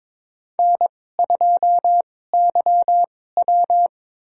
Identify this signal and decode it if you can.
Morse code, heard as a keyed tone, about 22 wpm, spaced as for 16 wpm, 700 Hz. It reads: N2YW
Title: USA Call Signs - 2x Character Spacing - 22wpm